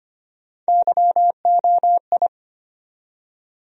Morse code, heard as a keyed tone, about 25 wpm, 700 Hz. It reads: YOI
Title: USA Call Sign Postfixes - 3 Letters 25wpm